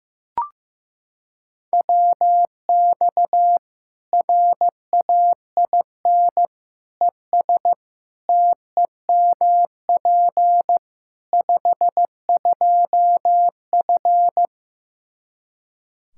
Morse code, heard as a keyed tone, about 16 wpm, 700 Hz. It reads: WX RAIN ES TEMP 52F
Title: Practice QSO - Exchange 2 - Info - 15wpm